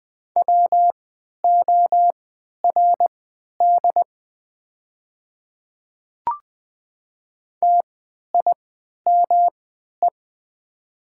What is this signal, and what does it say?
Morse code, heard as a keyed tone, about 20 wpm, spaced as for 11 wpm, 700 Hz. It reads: WORD TIME
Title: 3x Character Spacing - Top 100 Words - 20wpm